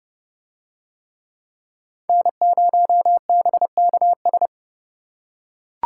Morse code, heard as a keyed tone, about 27 wpm, 700 Hz. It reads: N0BKS